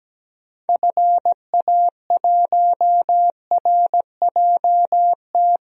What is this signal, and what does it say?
Morse code, heard as a keyed tone, about 17 wpm, 700 Hz. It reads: FA1RJT
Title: DXCC-100 Call Signs - 6 Characters - 17wpm